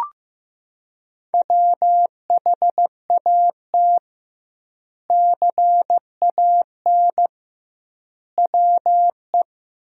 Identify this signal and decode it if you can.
Morse code, heard as a keyed tone, about 15 wpm, 700 Hz. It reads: WHAT CAN WE
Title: 2x Word-Spacing - Sentences from Top 100 Words - 15wpm